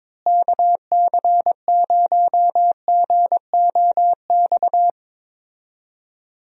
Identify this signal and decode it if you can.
Morse code, heard as a keyed tone, about 22 wpm, 700 Hz. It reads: KC0GOX
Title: USA Call Signs - 6 Characters - Repeated 3x - 22wpm